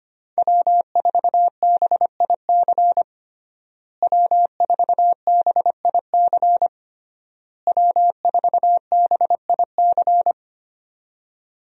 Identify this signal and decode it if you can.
Morse code, heard as a keyed tone, about 25 wpm, 700 Hz. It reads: W4BIC W4BIC W4BIC
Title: USA Call Signs - 5 Characters - Repeated 3x - 25wpm